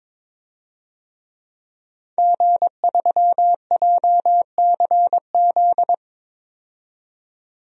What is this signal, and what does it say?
Morse code, heard as a keyed tone, about 22 wpm, 700 Hz. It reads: G3JCZ
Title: DXCC-100 Call Signs - 5 Characters - 22wpm